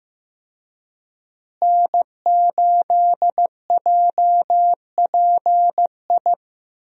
Morse code, heard as a keyed tone, about 15 wpm, 700 Hz. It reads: N8JPI
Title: USA Call Signs - 5 Characters - Repeated 2x - 15wpm